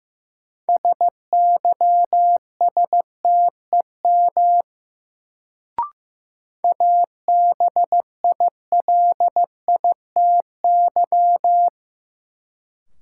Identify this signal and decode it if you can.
Morse code, heard as a keyed tone, about 15 wpm, 700 Hz. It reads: SYSTEM ABILITY
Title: Sentences from Top 1000 Words - Review Part 3 - New Word Drill - 15wpm